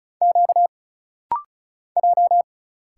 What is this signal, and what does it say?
Morse code, heard as a keyed tone, about 35 wpm, 700 Hz. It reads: Q J
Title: Single Letters - Rapid-Fire 35wpm